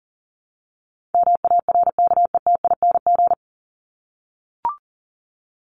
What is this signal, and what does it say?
Morse code, heard as a keyed tone, about 40 wpm, 700 Hz. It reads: MARKETING
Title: Top 751-1000 Words 40wpm